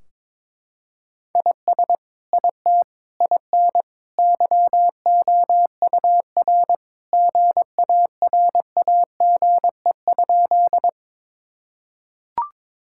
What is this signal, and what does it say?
Morse code, heard as a keyed tone, about 21 wpm, 700 Hz. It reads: IS IT IN YOUR GARAGE?
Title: Sentences from Top 1600 Words 22wpm